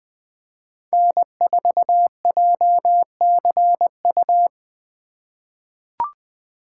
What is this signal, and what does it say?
Morse code, heard as a keyed tone, about 20 wpm, 700 Hz. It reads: N4JCU
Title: USA Call Signs - 5 Characters - 4x Character Spacing - 20wpm